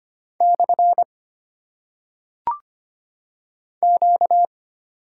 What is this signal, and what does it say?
Morse code, heard as a keyed tone, about 25 wpm, 700 Hz. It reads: / Q
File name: Top 90 QSO Elements - 8x Character Spacing - 25wpm